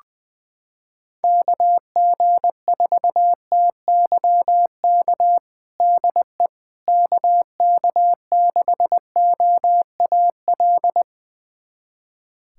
Morse code, heard as a keyed tone, about 20 wpm, 700 Hz. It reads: KG4TYK DE KK6OAL
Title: USA Callsign DE Callsign - 6 Characters - 20wpm